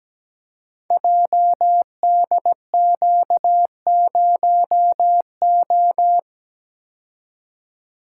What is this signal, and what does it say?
Morse code, heard as a keyed tone, about 17 wpm, 690 Hz. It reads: JDQ0O
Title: Code Groups - 5 Letter:Numbers - 17wpm